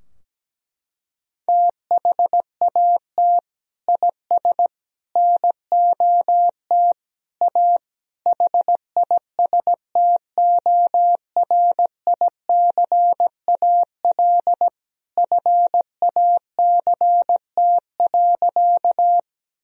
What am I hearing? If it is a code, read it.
Morse code, heard as a keyed tone, about 17 wpm, 700 Hz. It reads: THAT IS NOT A HISTORICAL FACT.
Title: Sentences from Top 1000 Words - Review Part 4 - 17wpm